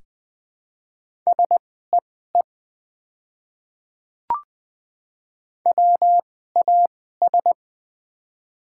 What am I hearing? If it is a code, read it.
Morse code, heard as a keyed tone, about 20 wpm, spaced as for 15 wpm, 700 Hz. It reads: SEE WAS
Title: 2x Character Spacing - Top 100 Words - 20wpm